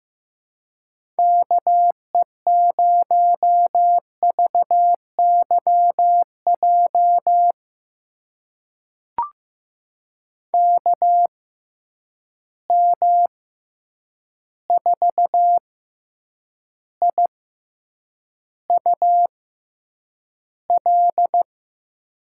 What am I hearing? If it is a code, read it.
Morse code, heard as a keyed tone, about 15 wpm, spaced as for 8 wpm, 700 Hz. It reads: KE0VYJ KM4IUL
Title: USA Call Signs - 6 Characters - 6x Character Spacing - 15wpm